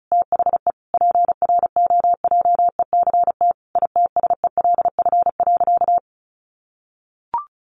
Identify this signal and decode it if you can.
Morse code, heard as a keyed tone, about 34 wpm, 700 Hz. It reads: THE PROJECT ITSELF.
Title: Sentences from Top 1400 Words 35wpm